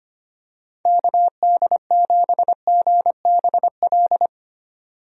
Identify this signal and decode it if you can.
Morse code, heard as a keyed tone, about 25 wpm, 700 Hz. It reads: KD7GBL
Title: USA Call Signs - 6 Characters - 25wpm